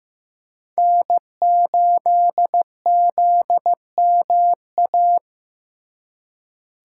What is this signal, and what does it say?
Morse code, heard as a keyed tone, about 18 wpm, 700 Hz. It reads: N8ZMA
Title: Speed Racing - USA 1x3 Call Signs 15wpm